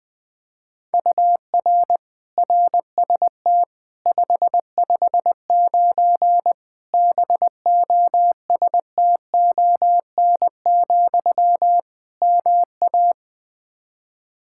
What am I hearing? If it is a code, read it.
Morse code, heard as a keyed tone, about 21 wpm, 700 Hz. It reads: UR RST 559 BOSTON, MA
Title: Practice QSO - Exchange 1 - Beginning - 20wpm